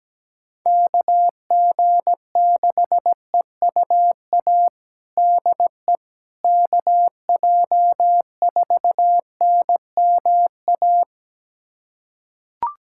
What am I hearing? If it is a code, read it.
Morse code, heard as a keyed tone, about 17 wpm, 700 Hz. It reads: KG6EUA DE KJ4NMA
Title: USA Callsign DE Callsign - 6 Characters - 17wpm